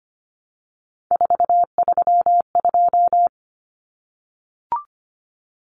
Morse code, heard as a keyed tone, about 25 wpm, 700 Hz. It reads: 432
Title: Common Cut Numbers - 3-Digit 25wpm